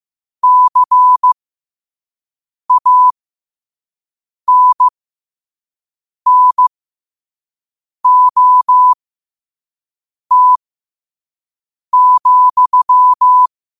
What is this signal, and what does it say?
Morse code, heard as a keyed tone, about 15 wpm, spaced as for 5 wpm, 1000 Hz. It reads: CANNOT,
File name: qotdfiles / 5wpm